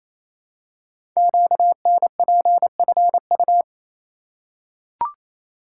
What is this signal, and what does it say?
Morse code, heard as a keyed tone, about 27 wpm, 700 Hz. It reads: QNPFU
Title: Code Groups - 5 Letters - 28wpm